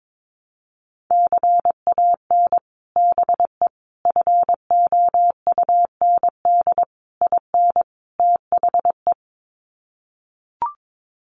Morse code, heard as a keyed tone, about 22 wpm, 700 Hz. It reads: CAN BE FOUND IN THE
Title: Sets of 5 Words Encore 22wpm